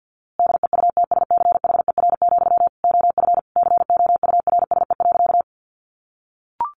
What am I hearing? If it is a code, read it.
Morse code, heard as a keyed tone, about 48 wpm, 700 Hz. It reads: DEUTSCHER, OF COURSE.
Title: A Sound Of Thunder 50wpm